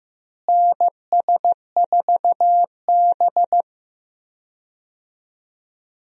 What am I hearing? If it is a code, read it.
Morse code, heard as a keyed tone, about 15 wpm, 700 Hz. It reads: NS4B
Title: USA Call Signs - 4 Characters - Repeated 3x - 15wpm